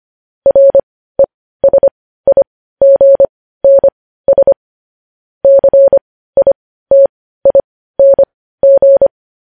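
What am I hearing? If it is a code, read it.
Morse code, heard as a keyed tone, about 25 wpm, spaced as for 15 wpm, 550 Hz. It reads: RESIGNS CITING
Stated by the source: Morse code news headlines sent at 15 wpm.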